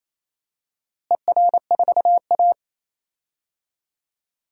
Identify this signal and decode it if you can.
Morse code, heard as a keyed tone, about 28 wpm, 700 Hz. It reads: ER4A
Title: DXCC-100 Call Signs - 4 Characters - Repeated 2x - 28wpm